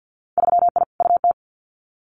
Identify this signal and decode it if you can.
Morse code, heard as a keyed tone, about 50 wpm, 700 Hz. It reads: 3I UT
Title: Winter Field Day Exchanges Encore 50wpm